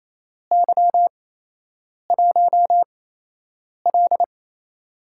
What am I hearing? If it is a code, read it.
Morse code, heard as a keyed tone, about 28 wpm, spaced as for 8 wpm, 700 Hz. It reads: Y1L
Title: USA Call Signs - 4 Characters - 8x Character Spacing - 28wpm